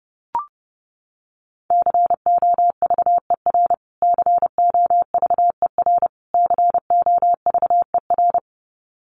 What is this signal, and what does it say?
Morse code, heard as a keyed tone, about 30 wpm, 700 Hz. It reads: COVER COVER COVER
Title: Top 500 Words Repeated 30wpm